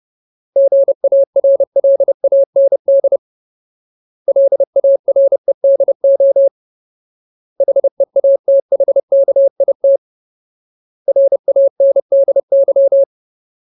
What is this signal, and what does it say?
Morse code, heard as a keyed tone, about 30 wpm, 550 Hz. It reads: GARLAND LAREDO HEATHKIT RANDY
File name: Ultimate Word List / 30WPM Ultimate